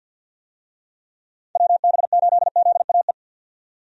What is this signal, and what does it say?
Morse code, heard as a keyed tone, about 51 wpm, 700 Hz. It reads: WB8ZAE
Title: USA Call Signs - 6 Characters - 50wpm